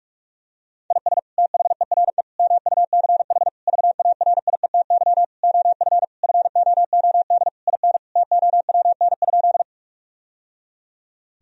Morse code, heard as a keyed tone, about 46 wpm, 700 Hz. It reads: IS THERE MUCH VARIETY OF FOOD IN TOWN?
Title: Sentences from Top 1000 Words - Review Part 4 - 45wpm